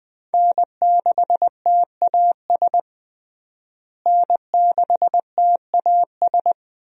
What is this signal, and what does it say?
Morse code, heard as a keyed tone, about 20 wpm, 700 Hz. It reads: N6TAS N6TAS
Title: USA Call Signs - 5 Characters - Repeated 2x - 20wpm